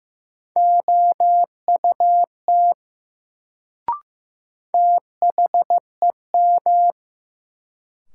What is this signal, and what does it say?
Morse code, heard as a keyed tone, about 15 wpm, 700 Hz. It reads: OUT THEM
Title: Sentences from Top 100 Words - New Word Drill - 15wpm